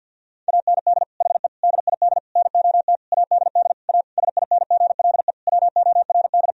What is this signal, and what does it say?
Morse code, heard as a keyed tone, about 52 wpm, 700 Hz. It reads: AND HE DID NOT ADD A SINGLE WORD
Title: Sentences from Top 1000 Words - Review Part 1 - 50wpm